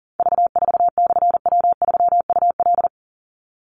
Morse code, heard as a keyed tone, about 37 wpm, 700 Hz. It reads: V4/W3UL
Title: Call Signs - 40wpm